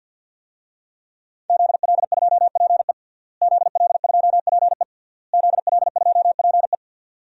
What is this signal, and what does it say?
Morse code, heard as a keyed tone, about 53 wpm, 700 Hz. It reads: ZL2PE ZL2PE ZL2PE
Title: DXCC-100 Call Signs - 5 Characters - Repeated 3x - 50wpm